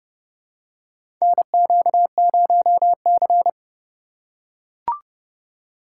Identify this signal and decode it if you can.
Morse code, heard as a keyed tone, about 31 wpm, 700 Hz. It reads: NQ0C
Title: USA Call Signs - 4 Characters - 6x Character Spacing - 30wpm